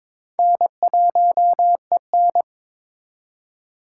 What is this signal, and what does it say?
Morse code, heard as a keyed tone, about 22 wpm, 700 Hz. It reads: N1EN
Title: K1USN SST QSO Practice - 22wpm